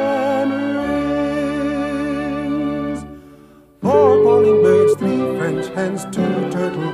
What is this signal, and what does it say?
Ringback tone (Europe, 425 Hz), with noise.
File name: LCJ0l6mBB1t7bjG5